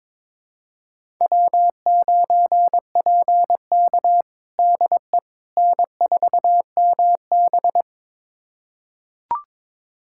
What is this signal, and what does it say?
Morse code, heard as a keyed tone, about 22 wpm, 700 Hz. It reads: W9PK DE N4MB
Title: USA Callsign DE Callsign - 4 Characters - 22wpm